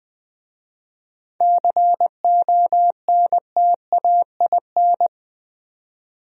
Morse code, heard as a keyed tone, about 20 wpm, 700 Hz. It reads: CONTAIN
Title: Speed-Racing Top 500 Words 17wpm